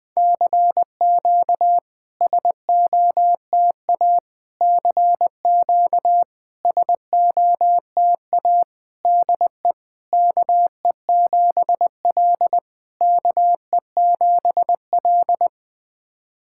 Morse code, heard as a keyed tone, about 20 wpm, 700 Hz. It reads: CQ SOTA CQ SOTA DE KE7L KE7L
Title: SOTA QSO Practice 20wpm